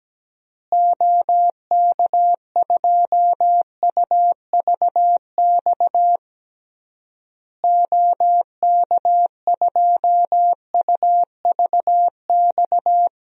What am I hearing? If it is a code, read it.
Morse code, heard as a keyed tone, about 17 wpm, 700 Hz. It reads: OK2UVX OK2UVX
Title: DXCC-100 Call Signs - 6 Characters - Repeated 3x - 17wpm